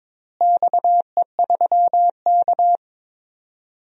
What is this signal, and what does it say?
Morse code, heard as a keyed tone, about 21 wpm, 700 Hz. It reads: XE3K
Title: DXCC-100 Call Signs - 4 Characters - Repeated 2x - 22wpm